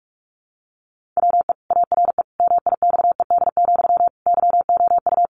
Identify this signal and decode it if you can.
Morse code, heard as a keyed tone, about 44 wpm, 700 Hz. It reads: WE ARE MIXED, YOU
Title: Sentences from Top 1500 Words 45wpm